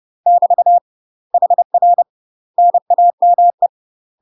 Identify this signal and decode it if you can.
Morse code, heard as a keyed tone, about 30 wpm, 700 Hz. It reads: <BT> HR NAME
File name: UK MASTER QSOs TRAINING/30WPM